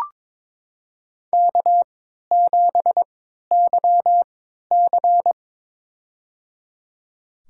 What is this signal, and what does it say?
Morse code, heard as a keyed tone, about 22 wpm, spaced as for 13 wpm, 700 Hz. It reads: K7YC
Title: USA Call Signs - 3x Character Spacing - 22wpm